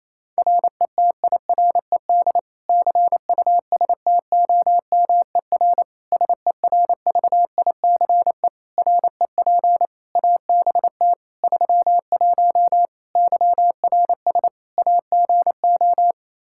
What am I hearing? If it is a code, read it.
Morse code, heard as a keyed tone, about 27 wpm, 700 Hz. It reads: RETIRED CUSTOMER SERVICE REP ABT 31 YRS AGO
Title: Practice QSO - Exchange 3 - More Info - 28wpm